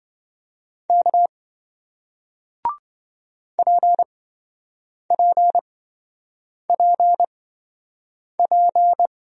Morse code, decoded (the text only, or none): K PPPP